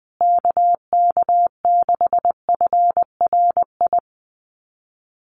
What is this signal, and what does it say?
Morse code, heard as a keyed tone, about 20 wpm, 700 Hz. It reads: KK6FRI
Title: USA Call Signs Repeated 2x 20wpm